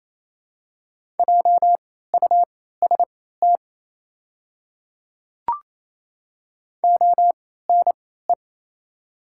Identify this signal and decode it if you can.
Morse code, heard as a keyed tone, about 28 wpm, spaced as for 16 wpm, 700 Hz. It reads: JUST ONE
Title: 3x Character Spacing - Top 300 Words - 28wpm